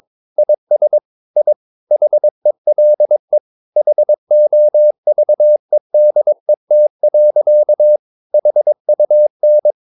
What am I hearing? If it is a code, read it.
Morse code, heard as a keyed tone, about 22 wpm, 600 Hz. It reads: IS I HELE HOVEDET. HUN
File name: Ferie QRS 22wpm